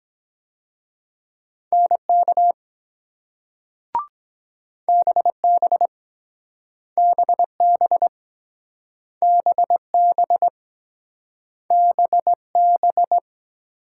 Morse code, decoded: NK BB BB BB BB

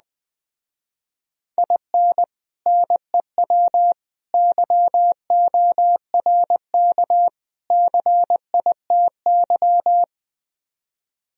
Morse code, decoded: IN NEW YORK CITY

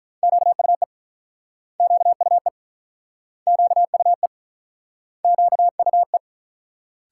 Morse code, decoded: QUE QUE QUE QUE

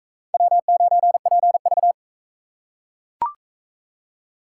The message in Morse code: W9PV

